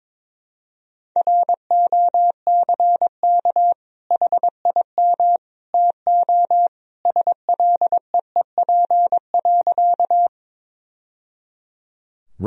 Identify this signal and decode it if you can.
Morse code, heard as a keyed tone, about 22 wpm, 700 Hz. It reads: ROCK HIM TO SLEEP.
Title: Sentences from Top 1000 Words - Review Part 2 - 22wpm